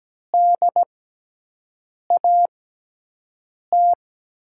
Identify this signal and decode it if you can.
Morse code, heard as a keyed tone, about 17 wpm, spaced as for 6 wpm, 700 Hz. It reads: DAT
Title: USA Call Signs - 6 Characters - 6x Character Spacing - 17wpm